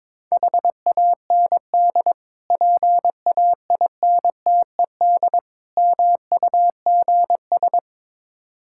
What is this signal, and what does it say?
Morse code, heard as a keyed tone, about 21 wpm, 700 Hz. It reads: HAND PAINTED MUGS
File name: Difficult Words with Hints - 22wpm